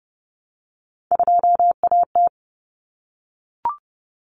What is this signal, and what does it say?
Morse code, heard as a keyed tone, about 30 wpm, 700 Hz. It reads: 2AT
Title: Common Cut Numbers - 3-Digit 30wpm